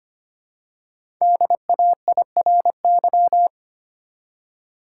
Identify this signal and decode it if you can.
Morse code, heard as a keyed tone, about 25 wpm, 700 Hz. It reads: DAIRY
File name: 5 Letter Words - First 3 Letters Given - 25wpm